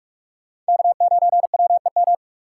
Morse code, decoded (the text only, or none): K9WEM